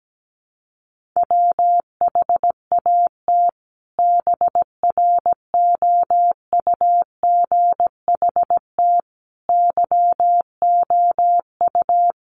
Morse code, heard as a keyed tone, about 17 wpm, 700 Hz. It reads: WHAT BROUGHT YOU
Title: Sentences from Top 500 Words 17wpm